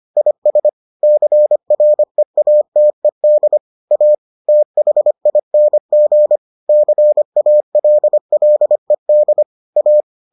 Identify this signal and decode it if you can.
Morse code, heard as a keyed tone, about 25 wpm, 600 Hz. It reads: IS CREATED A THING CALLED A